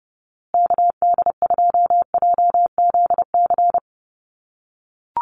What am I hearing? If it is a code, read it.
Morse code, heard as a keyed tone, about 30 wpm, 700 Hz. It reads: KD2JZC
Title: USA Call Signs Repeated 2x 30wpm